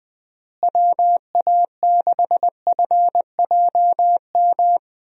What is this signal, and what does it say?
Morse code, heard as a keyed tone, about 20 wpm, 700 Hz. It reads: WA6FJM
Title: USA Call Signs - 6 Characters - Repeated 2x - 20wpm